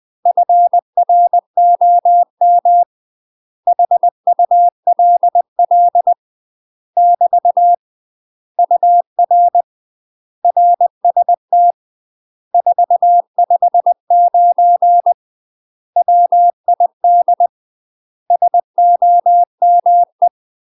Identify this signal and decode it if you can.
Morse code, heard as a keyed tone, about 20 wpm, 700 Hz. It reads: FROM HULL <BT> UR RST 459 WID SOME
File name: UK MASTER QSOs TRAINING / 20WPM / UK MASTER QSOs 0002